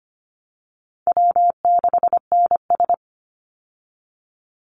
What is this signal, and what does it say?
Morse code, heard as a keyed tone, about 25 wpm, 700 Hz. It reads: W6NS
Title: Easy Call Signs Repeated 2x - 25wpm